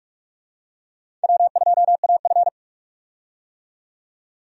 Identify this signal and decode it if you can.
Morse code, heard as a keyed tone, about 42 wpm, 700 Hz. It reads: W2AF
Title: USA Call Signs - 4 Characters - 45wpm